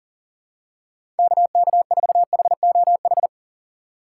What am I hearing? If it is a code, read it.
Morse code, heard as a keyed tone, about 40 wpm, 700 Hz. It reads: KK4HOH
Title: USA Call Signs - 6 Characters - 2x Character Spacing - 40wpm